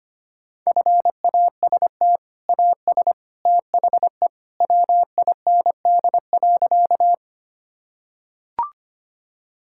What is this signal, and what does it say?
Morse code, heard as a keyed tone, about 25 wpm, 700 Hz. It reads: FAST AS THE WIND.
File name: practice / 25wpm / Sentences with 2-Word Phrases - 25wpm